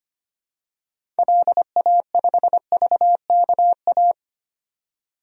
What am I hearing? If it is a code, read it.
Morse code, heard as a keyed tone, about 25 wpm, 700 Hz. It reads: LA5VKA
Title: DXCC-100 Call Signs - 6 Characters - 25wpm